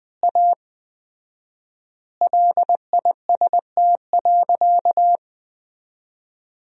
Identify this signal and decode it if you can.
Morse code, heard as a keyed tone, about 20 wpm, 700 Hz. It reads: A LIST.
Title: 4x Word-Spacing - Sentences from Top 400 Words - 20wpm